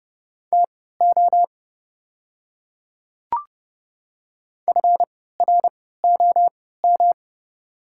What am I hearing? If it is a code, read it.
Morse code, heard as a keyed tone, about 30 wpm, spaced as for 17 wpm, 690 Hz. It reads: TO FROM
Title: 3x Character Spacing - Top 300 Words - 30wpm